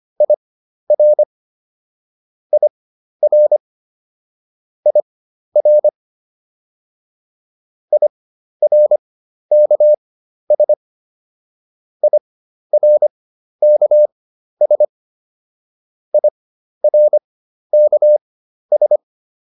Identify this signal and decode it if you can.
Morse code, heard as a keyed tone, about 25 wpm, spaced as for 12 wpm, 600 Hz. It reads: IR IR IR IRKS IRKS IRKS